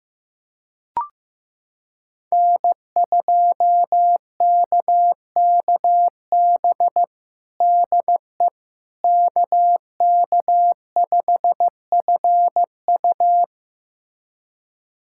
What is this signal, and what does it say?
Morse code, heard as a keyed tone, about 16 wpm, 700 Hz. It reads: N2KKB DE KK5FU
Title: USA Callsign DE Callsign 15wpm